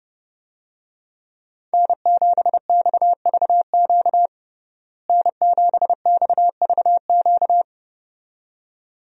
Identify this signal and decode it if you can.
Morse code, heard as a keyed tone, about 30 wpm, 700 Hz. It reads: N7XVQ N7XVQ
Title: USA Call Signs - 5 Characters - Repeated 2x - 30wpm